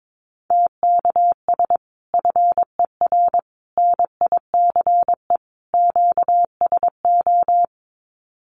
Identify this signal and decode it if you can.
Morse code, heard as a keyed tone, about 22 wpm, 700 Hz. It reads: TKS FER NICE QSO
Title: Common QSO Phrases 22wpm